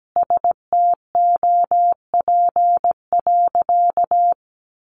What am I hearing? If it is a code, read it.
Morse code, heard as a keyed tone, about 17 wpm, 700 Hz. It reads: STOP.
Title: Sentences from Top 1300 Words 17wpm